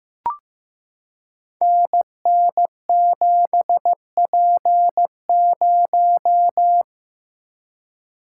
Morse code, heard as a keyed tone, about 15 wpm, 700 Hz. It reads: NN7P0
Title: Code Groups - 5 Letter:Numbers - 15wpm